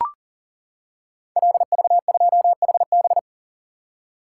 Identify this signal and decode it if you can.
Morse code, heard as a keyed tone, about 38 wpm, 700 Hz. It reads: LV2HB
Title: Code Groups - 5 Letter:Numbers - 40wpm